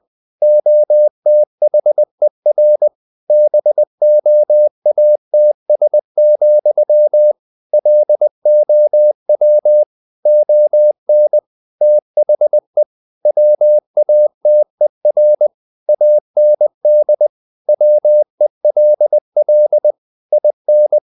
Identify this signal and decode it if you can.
Morse code, heard as a keyed tone, about 20 wpm, 600 Hz. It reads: OTHER BOATS, LOW ON THE WATER AND WELL IN
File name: the old man and the sea / the old man and the sea part 1 20wpm